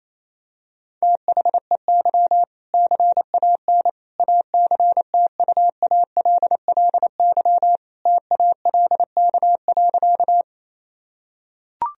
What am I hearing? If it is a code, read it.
Morse code, heard as a keyed tone, about 28 wpm, 700 Hz. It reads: THEY CAN ACTUALLY TALK.
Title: Sentences from Top 1000 Words - Review Part 4 - 28wpm